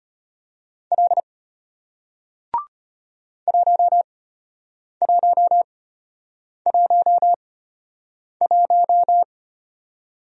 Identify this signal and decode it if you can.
Morse code, heard as a keyed tone, about 33 wpm, spaced as for 8 wpm, 700 Hz. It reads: L 1111